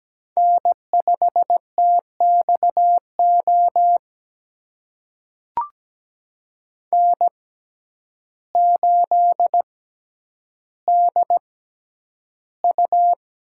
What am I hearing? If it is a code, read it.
Morse code, heard as a keyed tone, about 17 wpm, spaced as for 9 wpm, 700 Hz. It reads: N5TXO N8DU
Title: USA Call Signs - 5 Characters - 6x Character Spacing - 17wpm